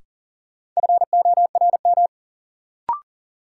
Morse code, heard as a keyed tone, about 40 wpm, 700 Hz. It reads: FORM
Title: Sentences from Top 1000 Words - Review Part 1 - New Word Drill - 40wpm